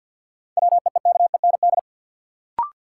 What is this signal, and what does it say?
Morse code, heard as a keyed tone, about 53 wpm, 700 Hz. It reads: WEEKEND
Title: Sets of 5 Words - New Word Drill - 50wpm